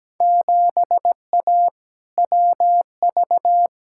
Morse code, heard as a keyed tone, about 17 wpm, 700 Hz. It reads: 7A WV
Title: Field Day Exchanges Encore - 17wpm